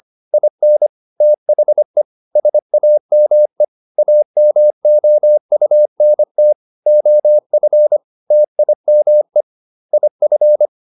Morse code, heard as a keyed tone, about 25 wpm, 600 Hz. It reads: IN THE SAME AMOUNT OF TIME IF
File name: SS220 25